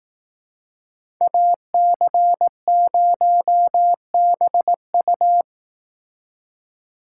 Morse code, heard as a keyed tone, about 19 wpm, 700 Hz. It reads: AC0BU
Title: Speed Racing - USA 2x2 Call Signs 15wpm